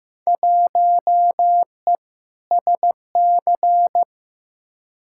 Morse code, heard as a keyed tone, about 15 wpm, 700 Hz. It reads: 1E SC
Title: Field Day Exchanges - 15wpm